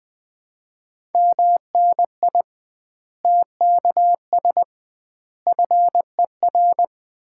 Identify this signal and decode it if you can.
Morse code, heard as a keyed tone, about 21 wpm, 700 Hz. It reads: MNI TKS FER
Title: Common QSO Phrases - 2x Word-Spacing - 20wpm